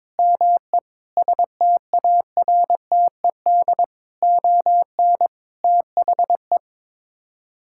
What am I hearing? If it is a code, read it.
Morse code, heard as a keyed tone, about 21 wpm, 700 Hz. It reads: ME STARTED ON THE
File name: practice / 22wpm / Sets of 8 Words - 22wpm